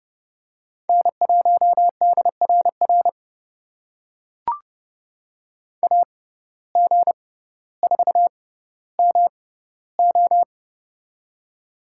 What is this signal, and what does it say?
Morse code, heard as a keyed tone, about 30 wpm, spaced as for 15 wpm, 700 Hz. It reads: N1DRR AG4MO
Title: USA Call Signs - 5 Characters - 6x Character Spacing - 30wpm